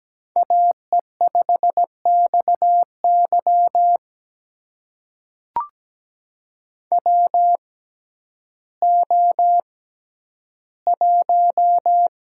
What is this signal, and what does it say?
Morse code, heard as a keyed tone, about 17 wpm, spaced as for 10 wpm, 700 Hz. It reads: AE5XY WO1
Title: USA Call Signs - 6x Character Spacing - 17wpm